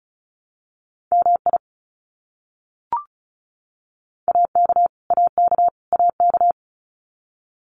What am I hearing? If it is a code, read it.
Morse code, heard as a keyed tone, about 34 wpm, 700 Hz. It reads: MI AK AK AK
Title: US States Repeated 3x 35wpm